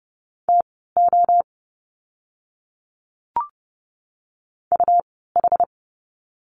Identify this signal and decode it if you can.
Morse code, heard as a keyed tone, about 30 wpm, spaced as for 17 wpm, 700 Hz. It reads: TO UH
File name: Lesson 17f - Letter u - Words 30wpm